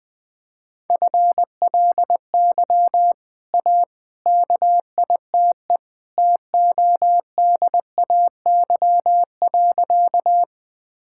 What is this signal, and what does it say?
Morse code, heard as a keyed tone, about 20 wpm, 700 Hz. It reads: FLY A KITE TODAY.
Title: Easy Sentences with Hints - 20wpm